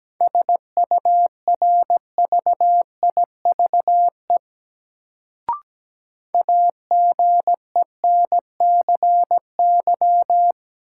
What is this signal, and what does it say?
Morse code, heard as a keyed tone, about 17 wpm, 700 Hz. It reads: SURVIVE AGENCY
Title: Sentences from Top 1300 Words - New Word Drill - 17wpm